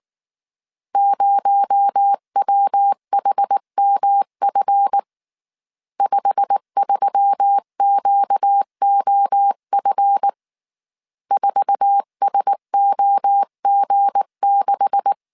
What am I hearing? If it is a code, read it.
Morse code, heard as a keyed tone, about 19 wpm, 800 Hz. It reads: 0WHMF 53QOF 4SOG6